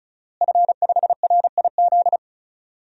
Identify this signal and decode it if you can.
Morse code, heard as a keyed tone, about 36 wpm, 700 Hz. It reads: F5RIZ
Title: DXCC-100 Call Signs - 5 Characters - Repeated 3x - 35wpm